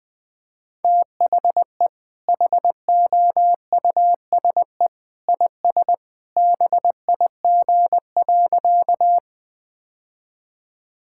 Morse code, heard as a keyed tone, about 20 wpm, 700 Hz. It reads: THE HOUSE IS BIG.
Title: Sentences with 2-Word Phrases - Spoken Beginning - 20wpm